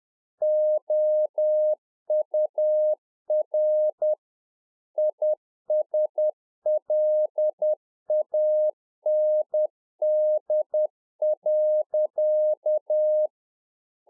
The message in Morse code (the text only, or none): OUR ISLAND.